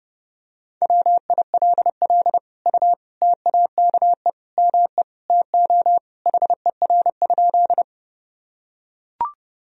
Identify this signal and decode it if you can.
Morse code, heard as a keyed tone, about 32 wpm, 700 Hz. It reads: WILL U TAKE ME TO HER?
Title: Sentences from Top 200 Words and Abbreviations 30wpm